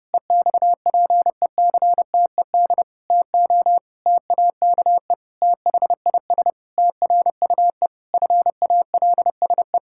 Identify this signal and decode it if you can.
Morse code, heard as a keyed tone, about 30 wpm, 700 Hz. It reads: EXPECTED TO TAKE THIS TRUE FALSE
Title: Pangram Challenge v2 - 30wpm